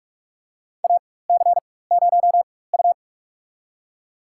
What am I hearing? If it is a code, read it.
Morse code, heard as a keyed tone, about 46 wpm, spaced as for 21 wpm, 700 Hz. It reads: AC0U
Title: USA Call Signs - 4 Characters - 4x Character Spacing - 45wpm